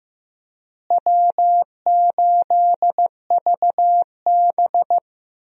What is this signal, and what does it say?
Morse code, heard as a keyed tone, about 16 wpm, 700 Hz. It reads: W8VB
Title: USA Call Signs - 4 Characters - Repeated 3x - 15wpm